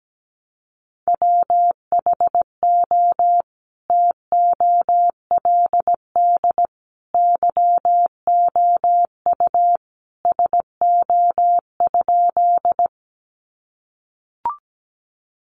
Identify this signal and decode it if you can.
Morse code, heard as a keyed tone, about 17 wpm, 700 Hz. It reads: WHO TOLD YOU SO?
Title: Sentences from Top 400 Words 17wpm